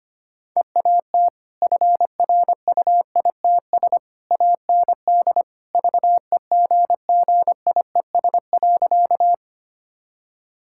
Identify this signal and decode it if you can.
Morse code, heard as a keyed tone, about 25 wpm, 700 Hz. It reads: EAT FRUITS AND VEGGIES.
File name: Easy Sentences with Hints - 25wpm